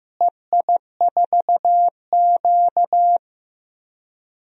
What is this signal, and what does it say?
Morse code, heard as a keyed tone, about 15 wpm, 700 Hz. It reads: EI4Q
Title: DXCC-100 Call Signs - 4 Characters - Repeated 3x - 15wpm